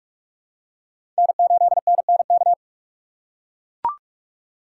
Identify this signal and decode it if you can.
Morse code, heard as a keyed tone, about 43 wpm, 700 Hz. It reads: N8NNK
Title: USA Call Signs - 5 Characters - 4x Character Spacing - 45wpm